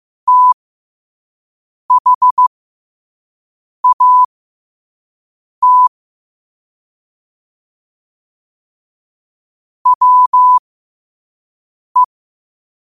Morse code, heard as a keyed tone, about 15 wpm, spaced as for 5 wpm, 1000 Hz. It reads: THAT WE
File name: qotdfiles / 5wpm